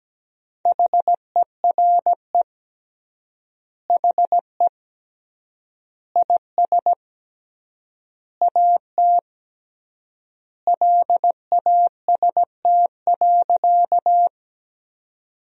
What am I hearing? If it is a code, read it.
Morse code, heard as a keyed tone, about 17 wpm, 700 Hz. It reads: HERE HE IS AT LAST.
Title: 3x Word-Spacing - Sentences from Top 300 Words - 17wpm